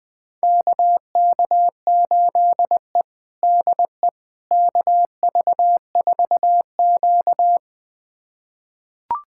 Morse code, heard as a keyed tone, about 21 wpm, 700 Hz. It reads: KK8E DE KV4Q
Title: USA Callsign DE Callsign - 4 Characters - 20wpm